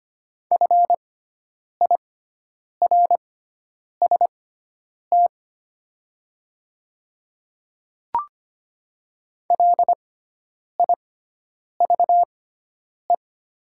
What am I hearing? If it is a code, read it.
Morse code, heard as a keyed tone, about 25 wpm, spaced as for 9 wpm, 690 Hz. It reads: FIRST LIVE
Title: 6x Character Spacing - Top 200 Words - 25wpm